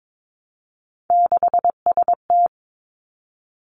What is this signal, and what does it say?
Morse code, heard as a keyed tone, about 22 wpm, 700 Hz. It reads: 6ST
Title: Three Challenging Letter-Number Combinations 22wpm